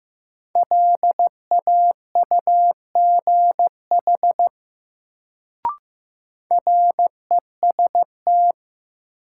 Morse code, heard as a keyed tone, about 15 wpm, 700 Hz. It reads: LAUGH REST
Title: Sentences from Top 500 Words - New Word Drill - 15wpm